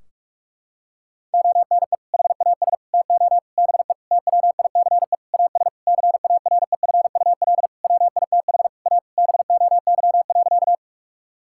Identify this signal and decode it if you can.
Morse code, heard as a keyed tone, about 46 wpm, 700 Hz. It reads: ONE HAS TO BE TWICE AS CAREFUL WITH A BOY.
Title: Sentences from Top 1600 Words 45wpm